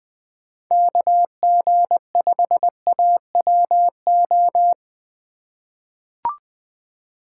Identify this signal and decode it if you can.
Morse code, heard as a keyed tone, about 20 wpm, 700 Hz. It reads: KG5AWO
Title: USA Call Signs - 3x Character Spacing - 20wpm